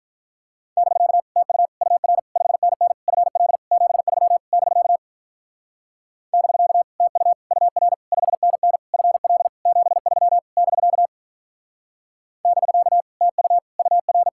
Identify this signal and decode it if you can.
Morse code, heard as a keyed tone, about 44 wpm, 700 Hz. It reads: <BK> TU UR 5NN FL 73 <BK> <BK> TU UR 5NN FL 73 <BK> <BK> TU UR
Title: Speed-Racing Common QSO Phrases - 35wpm